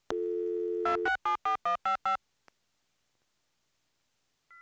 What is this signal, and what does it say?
DTMF, heard as telephone touch tones, tones of about 120 ms. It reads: 2674155
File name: machine sound effects / dial-up-modem-3